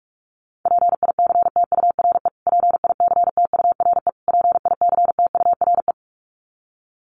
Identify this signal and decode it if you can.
Morse code, heard as a keyed tone, about 44 wpm, 700 Hz. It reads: PICTURE PICTURE PICTURE
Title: Top 500 Words Repeated 45wpm